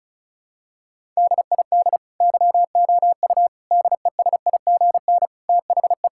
Morse code, heard as a keyed tone, about 36 wpm, 700 Hz. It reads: DID YOU DESIGN THE
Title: Sentences from Top 1000 Words - Review Part 5 - 35wpm